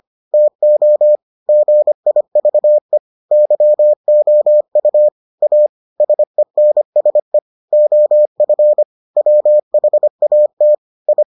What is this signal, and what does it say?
Morse code, heard as a keyed tone, about 25 wpm, 600 Hz. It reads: TO GIVE YOU A SENSE OF WHAT I